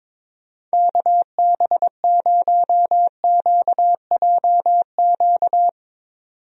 Morse code, heard as a keyed tone, about 21 wpm, 700 Hz. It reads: KB0QJQ
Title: USA Call Signs - 6 Characters - 22wpm